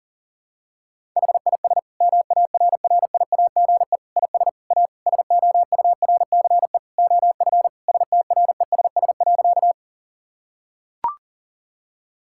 Morse code, heard as a keyed tone, about 37 wpm, 700 Hz. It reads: HIS MARRIAGE IS A SOURCE OF STRESS.
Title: Sentences from Top 1000 Words - Review Part 6 - 40wpm